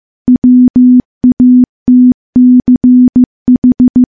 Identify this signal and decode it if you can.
Morse code, heard as a keyed tone, about 15 wpm, 250 Hz. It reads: WATCH
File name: morse-training-15WPM